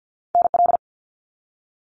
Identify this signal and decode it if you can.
Morse code, heard as a keyed tone, about 50 wpm, 700 Hz. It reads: NL
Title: Canadian Provinces and Territories - Repeated 3x - 50wpm